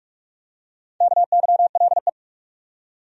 Morse code, heard as a keyed tone, about 46 wpm, 700 Hz. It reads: KYLE
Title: USA Common Names - 45wpm